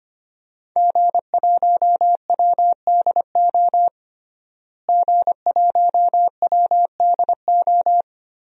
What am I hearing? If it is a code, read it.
Morse code, heard as a keyed tone, about 25 wpm, 700 Hz. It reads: G1WDO G1WDO
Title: DXCC-100 Call Signs - 5 Characters - Repeated 2x - 25wpm